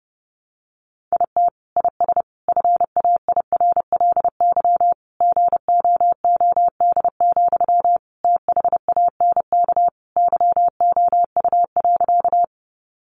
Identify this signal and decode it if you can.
Morse code, heard as a keyed tone, about 28 wpm, 700 Hz. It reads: IT IS FAIRLY GOOD, THANK YOU.
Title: Sentences from Top 1100 Words 30wpm